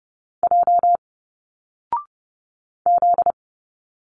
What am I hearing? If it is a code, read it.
Morse code, heard as a keyed tone, about 30 wpm, 700 Hz. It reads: J Z
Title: Single Letters 30wpm